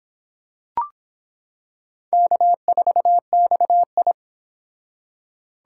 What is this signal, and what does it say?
Morse code, heard as a keyed tone, about 26 wpm, 700 Hz. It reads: K4XI